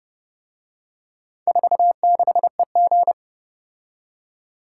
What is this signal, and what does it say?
Morse code, heard as a keyed tone, about 29 wpm, 700 Hz. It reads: 46EG